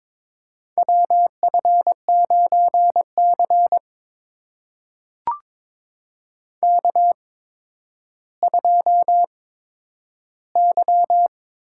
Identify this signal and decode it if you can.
Morse code, heard as a keyed tone, about 22 wpm, spaced as for 11 wpm, 700 Hz. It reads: WF9C K2Y
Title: USA Call Signs - 4 Characters - 8x Character Spacing - 22wpm